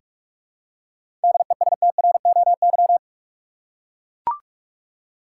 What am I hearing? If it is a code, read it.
Morse code, heard as a keyed tone, about 43 wpm, 700 Hz. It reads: DESTROY
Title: Seven Letter Words - 45wpm